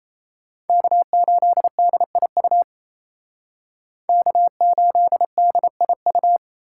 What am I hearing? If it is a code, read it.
Morse code, heard as a keyed tone, about 36 wpm, 700 Hz. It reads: K8DIU K8DIU